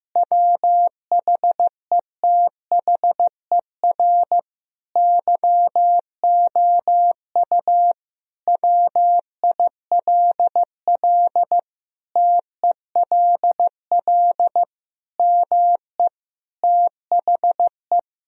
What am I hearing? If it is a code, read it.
Morse code, heard as a keyed tone, about 15 wpm, 700 Hz. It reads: WHETHER YOU WILL TELL ME THE
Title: Sentences from Top 1000 Words - Review Part 3 - 15wpm